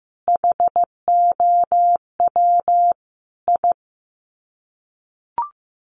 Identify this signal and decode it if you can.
Morse code, heard as a keyed tone, about 15 wpm, 700 Hz. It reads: HOW I
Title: Sets of 5 Words Encore2 15wpm